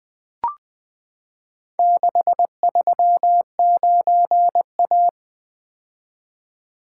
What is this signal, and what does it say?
Morse code, heard as a keyed tone, about 20 wpm, 700 Hz. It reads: 639A
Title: Four Letter-Number Combinations - 20wpm